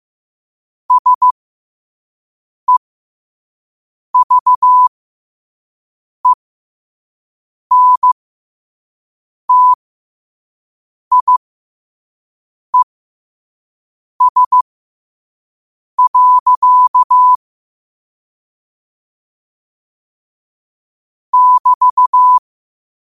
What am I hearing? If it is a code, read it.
Morse code, heard as a keyed tone, about 15 wpm, spaced as for 5 wpm, 1000 Hz. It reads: SEVENTIES. <BT>
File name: qotdfiles / 5wpm